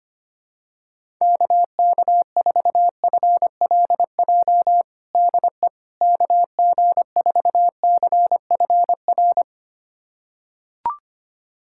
Morse code, heard as a keyed tone, about 25 wpm, 700 Hz. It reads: KK4FLJ DE KG4CFR
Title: USA Callsign DE Callsign - 6 Characters - 25wpm